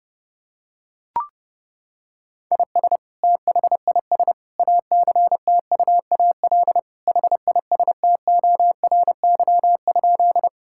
Morse code, heard as a keyed tone, about 32 wpm, 700 Hz. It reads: IS THIS ACTUAL HISTORY?
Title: Sentences from Top 1000 Words - Review Part 5 - 30wpm